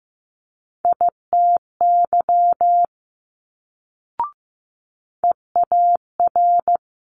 Morse code, heard as a keyed tone, about 15 wpm, 700 Hz. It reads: ITY EAR
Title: Multi-letter Phonemes Quick Fire 15wpm